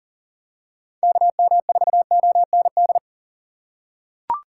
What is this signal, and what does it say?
Morse code, heard as a keyed tone, about 38 wpm, 700 Hz. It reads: KM4OND
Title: USA Call Signs - 3x Character Spacing - 40wpm